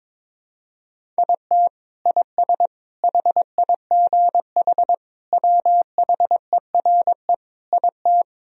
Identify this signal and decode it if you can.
Morse code, heard as a keyed tone, about 22 wpm, 700 Hz. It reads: IT IS HIGH WHERE IT
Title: Sentences from Top 1000 Words - Review Part 1 - 22wpm